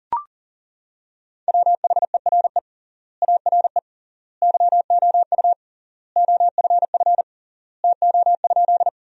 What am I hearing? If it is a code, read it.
Morse code, heard as a keyed tone, about 40 wpm, 700 Hz. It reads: WHERE ARE YOU OFF TO?
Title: 3x Word-Spacing - Sentences from Top 200 Words - 40wpm